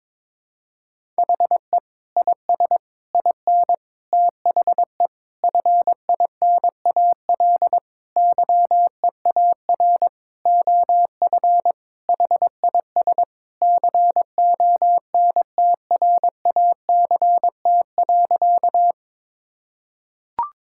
Morse code, heard as a keyed tone, about 22 wpm, 700 Hz. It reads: HE IS IN THE FINAL YEAR OF HIS CONTRACT.
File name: practice / 22wpm / Sentences with 8-Word Phrases - 22wpm